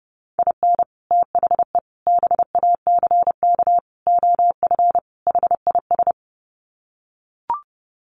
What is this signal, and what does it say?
Morse code, heard as a keyed tone, about 30 wpm, 700 Hz. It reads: IN THE BACK OF HIS
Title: Sets of 5 Words Encore2 30wpm